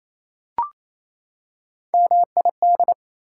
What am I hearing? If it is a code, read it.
Morse code, heard as a keyed tone, about 28 wpm, 700 Hz. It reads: MID
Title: Prefixes and Suffixes 28wpm